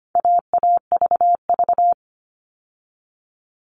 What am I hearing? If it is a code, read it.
Morse code, heard as a keyed tone, about 25 wpm, 700 Hz. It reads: AAVV
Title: All Cut Numbers - 4-Digit 25wpm